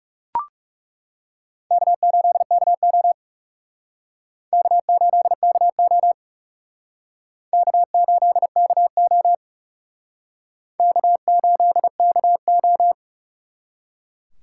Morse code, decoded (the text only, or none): K8KO K8KO K8KO K8KO